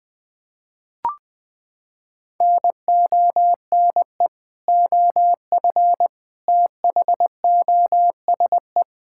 Morse code, heard as a keyed tone, about 20 wpm, 700 Hz. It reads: NONE OF THOSE
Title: Sentences with 2-Word Phrases - 20wpm